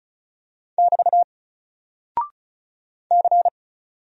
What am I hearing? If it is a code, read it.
Morse code, heard as a keyed tone, about 36 wpm, 700 Hz. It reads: <BT> C
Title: Single Character (Letter:Number:Punctuation:Prosign) Alt 35wpm